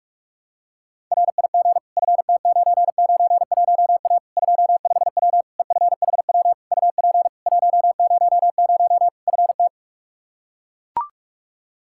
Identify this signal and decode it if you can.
Morse code, heard as a keyed tone, about 41 wpm, 700 Hz. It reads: RIG FT991A 25W EFHW UP 100 FT
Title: Practice QSO - Exchange 2 - Info - 45wpm